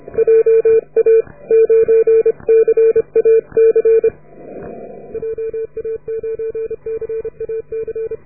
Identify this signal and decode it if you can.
Morse code, heard as a keyed tone, about 29 wpm, 450 Hz. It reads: JA9CAC JA9CAC